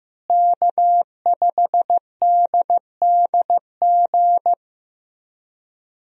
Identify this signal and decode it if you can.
Morse code, heard as a keyed tone, about 15 wpm, 700 Hz. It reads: K5DDG
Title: USA Call Signs - 5 Characters - Repeated 2x - 15wpm